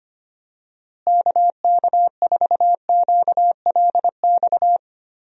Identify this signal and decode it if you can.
Morse code, heard as a keyed tone, about 25 wpm, 700 Hz. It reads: KK4QLX
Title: USA Call Signs - 6 Characters - Repeated 3x - 25wpm